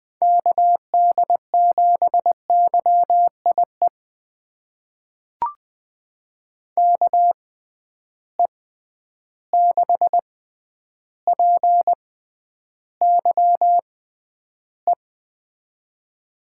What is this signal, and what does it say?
Morse code, heard as a keyed tone, about 20 wpm, spaced as for 10 wpm, 700 Hz. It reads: KD7YIE KE6PYE
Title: USA Call Signs - 6 Characters - 6x Character Spacing - 20wpm